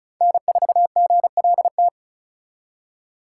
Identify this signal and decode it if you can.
Morse code, heard as a keyed tone, about 46 wpm, 700 Hz. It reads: N4GLT